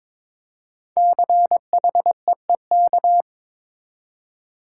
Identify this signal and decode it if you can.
Morse code, heard as a keyed tone, about 22 wpm, 700 Hz. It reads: CHEEK
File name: 5 Letter Words - First 3 Letters Given - 22wpm